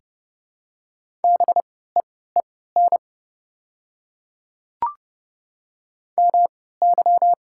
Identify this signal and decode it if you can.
Morse code, heard as a keyed tone, about 32 wpm, spaced as for 17 wpm, 700 Hz. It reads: BEEN MY
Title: 3x Character Spacing - Top 300 Words - 30wpm